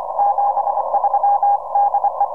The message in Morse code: 73 DE